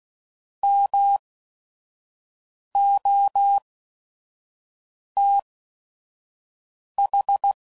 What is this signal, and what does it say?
Morse code, heard as a keyed tone, about 16 wpm, spaced as for 5 wpm, 800 Hz. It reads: MOTH